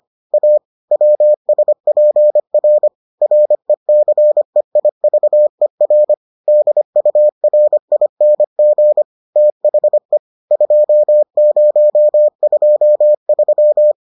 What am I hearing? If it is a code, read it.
Morse code, heard as a keyed tone, about 25 wpm, 600 Hz. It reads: A WSPR RECEIVER DURING THE 2023